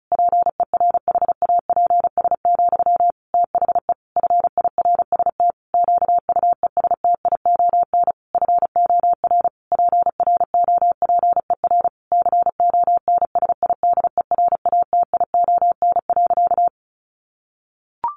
Morse code, heard as a keyed tone, about 34 wpm, 700 Hz. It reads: PERHAPS, THE FIRST QUESTION FOR PROPER CONSIDERATION.
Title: Sentences from Top 1500 Words 35wpm